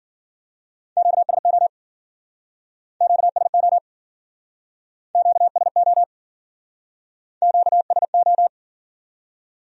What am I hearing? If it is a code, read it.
Morse code, heard as a keyed tone, about 52 wpm, 700 Hz. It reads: QSO QSO QSO QSO